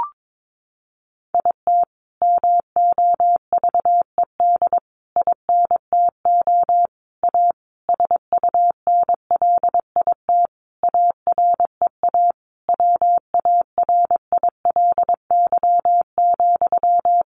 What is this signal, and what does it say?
Morse code, heard as a keyed tone, about 22 wpm, 700 Hz. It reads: IT MOVED INTO A SUNLIT AREA WARILY,
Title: A Sound Of Thunder 22wpm